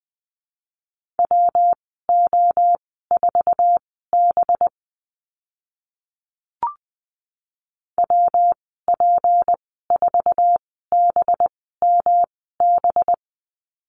Morse code, heard as a keyed tone, about 20 wpm, spaced as for 14 wpm, 700 Hz. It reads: WO4B WP4BMB
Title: 186 Lesson 30g - Letter b - Call Signs 20wpm